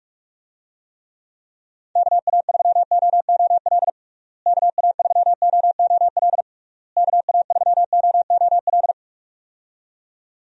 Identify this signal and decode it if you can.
Morse code, heard as a keyed tone, about 41 wpm, 700 Hz. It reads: KA3OOL KA3OOL KA3OOL
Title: USA Call Signs - 6 Characters - Repeated 3x - 45wpm